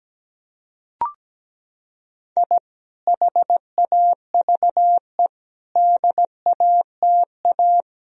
Morse code, heard as a keyed tone, about 17 wpm, 700 Hz. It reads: I HAVE DATA
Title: Sentences from Top 1000 Words - Review Part 3 - 17wpm